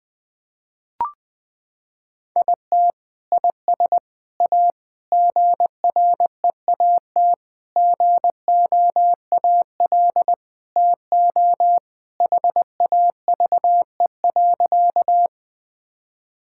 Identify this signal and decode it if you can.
Morse code, heard as a keyed tone, about 21 wpm, 700 Hz. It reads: IT IS A GREAT GOAL TO HAVE.
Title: Sentences from Top 1000 Words - Review Part 5 - 20wpm